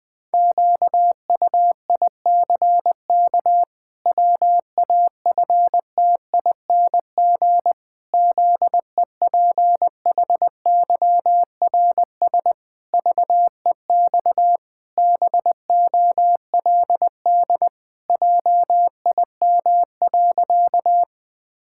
Morse code, heard as a keyed tone, about 20 wpm, 700 Hz. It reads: QUICK WAFTING ZEPHYRS VEX BOLD JIM.
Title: Pangram Challenge v2 - 20wpm